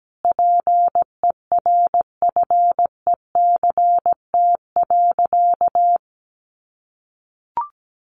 Morse code, heard as a keyed tone, about 17 wpm, 700 Hz. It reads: PERFECT.
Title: Sentences from Top 1300 Words 17wpm